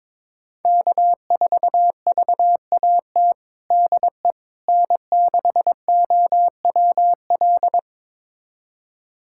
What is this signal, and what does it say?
Morse code, heard as a keyed tone, about 22 wpm, 700 Hz. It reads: K4VAT DE N6OWL
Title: USA Callsign DE Callsign - 5 Characters - 22wpm